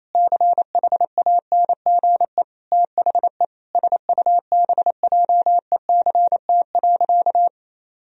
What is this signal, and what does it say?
Morse code, heard as a keyed tone, about 27 wpm, 700 Hz. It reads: CHANGE THE SUBJECT.
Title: Sentences from Top 1000 Words - Review Part 1 - 28wpm